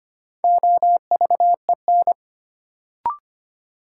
Morse code, heard as a keyed tone, about 25 wpm, 700 Hz. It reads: OVEN
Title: Sentences from Top 1200 Words - New Word Drill Alt - 25wpm